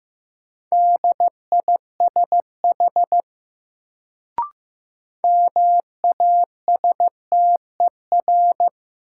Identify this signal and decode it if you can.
Morse code, heard as a keyed tone, about 15 wpm, 700 Hz. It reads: DISH MASTER
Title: Sentences from Top 1300 Words - New Word Drill Alt - 15wpm